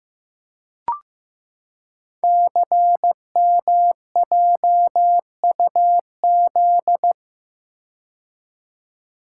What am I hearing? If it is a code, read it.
Morse code, heard as a keyed tone, about 15 wpm, 700 Hz. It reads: CMJUZ
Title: Code Groups - 5 Letters - 15wpm